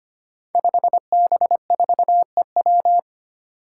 Morse code, heard as a keyed tone, about 25 wpm, 700 Hz. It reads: 5B4EW
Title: Five Challenging Letter-Number Combinations 25wpm